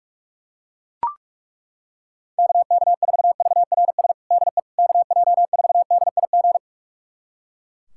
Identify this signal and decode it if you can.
Morse code, heard as a keyed tone, about 42 wpm, 700 Hz. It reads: KK4VRS DE KJ4DIG
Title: USA Callsign DE Callsign - 6 Characters - 45wpm